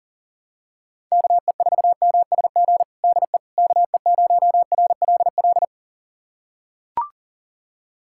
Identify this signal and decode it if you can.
Morse code, heard as a keyed tone, about 40 wpm, 700 Hz. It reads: KE4MSG DE KE0RLL
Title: USA Callsign DE Callsign - 6 Characters - 40wpm